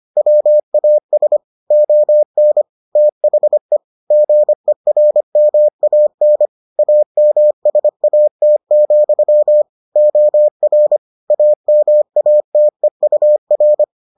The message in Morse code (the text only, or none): WAS ON THE GERMAN AMSAT, OR AMATEUR